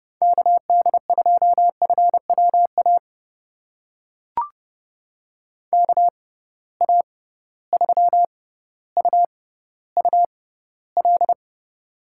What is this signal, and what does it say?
Morse code, heard as a keyed tone, about 30 wpm, spaced as for 15 wpm, 700 Hz. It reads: KD2FWA KA3UUL
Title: USA Call Signs - 6 Characters - 6x Character Spacing - 30wpm